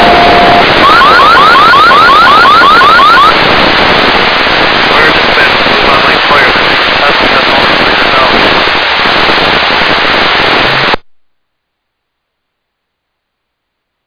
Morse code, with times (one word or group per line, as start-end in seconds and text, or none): none